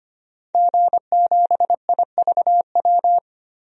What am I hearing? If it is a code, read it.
Morse code, heard as a keyed tone, about 25 wpm, 700 Hz. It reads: G7IVW
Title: DXCC-100 Call Signs - 5 Characters - Repeated 3x - 25wpm